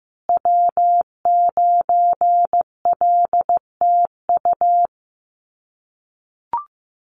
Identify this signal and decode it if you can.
Morse code, heard as a keyed tone, about 15 wpm, 700 Hz. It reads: W9LTU
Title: USA Call Signs Repeated 3x 15wpm